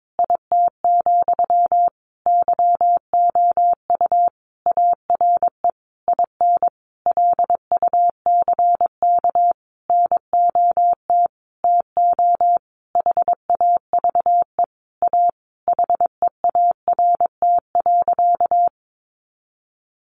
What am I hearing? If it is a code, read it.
Morse code, heard as a keyed tone, about 22 wpm, 700 Hz. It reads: IT, YOU ARE IN LUCK NOT TO HAVE A HEART.
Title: Sentences from Top 1500 Words 22wpm